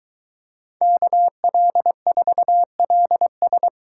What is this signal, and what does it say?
Morse code, heard as a keyed tone, about 22 wpm, 700 Hz. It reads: KL4LS